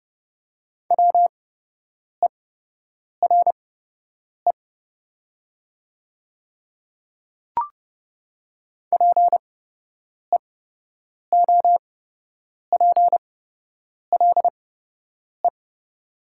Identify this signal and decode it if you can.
Morse code, heard as a keyed tone, about 30 wpm, spaced as for 8 wpm, 700 Hz. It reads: WERE PEOPLE
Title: 8x Character Spacing - Top 100 Words - 30wpm